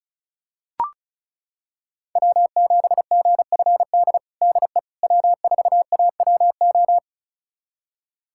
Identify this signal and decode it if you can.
Morse code, heard as a keyed tone, about 36 wpm, 700 Hz. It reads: W7GFD DE W4AWO
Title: USA Callsign DE Callsign - 5 Characters - 35wpm